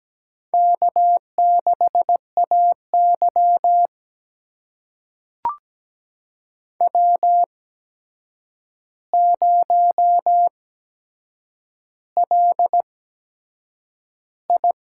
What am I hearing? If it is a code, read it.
Morse code, heard as a keyed tone, about 17 wpm, spaced as for 7 wpm, 700 Hz. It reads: K6AY W0LI
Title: USA Call Signs - 4 Characters - 8x Character Spacing - 17wpm